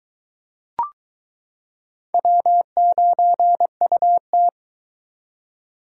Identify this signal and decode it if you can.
Morse code, heard as a keyed tone, about 22 wpm, 700 Hz. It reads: W9UT